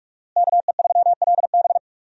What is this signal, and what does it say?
Morse code, heard as a keyed tone, about 46 wpm, 700 Hz. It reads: KE3LB
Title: USA Call Signs - 5 Characters - Repeated 3x - 45wpm